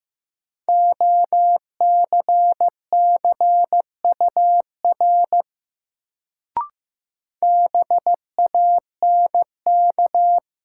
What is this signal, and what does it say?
Morse code, heard as a keyed tone, about 15 wpm, 700 Hz. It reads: OCCUR BANK
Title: Sentences from Top 900 Words - New Word Drill - 15wpm